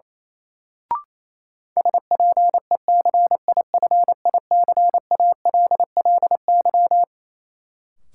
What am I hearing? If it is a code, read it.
Morse code, heard as a keyed tone, about 28 wpm, 700 Hz. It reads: SPECIFICALLY
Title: Sentences from Top 1100 Words - New Word Drill Alt - 28wpm